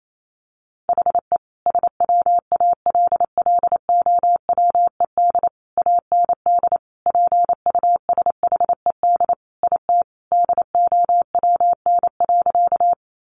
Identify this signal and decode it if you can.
Morse code, heard as a keyed tone, about 28 wpm, 700 Hz. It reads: HE SWALLOWED AND PUSHED IT DOWN.
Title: A Sound Of Thunder 28wpm